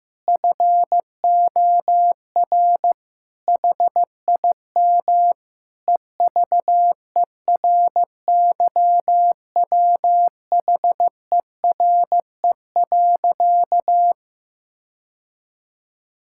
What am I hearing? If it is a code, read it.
Morse code, heard as a keyed tone, about 15 wpm, 700 Hz. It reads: FOR HIM EVERYWHERE.
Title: Sentences from Top 1600 Words 15wpm